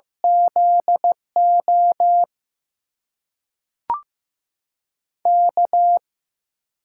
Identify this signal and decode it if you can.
Morse code, heard as a keyed tone, about 15 wpm, 700 Hz. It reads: ZO K
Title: USA Call Signs - 6 Characters - 8x Character Spacing - 15wpm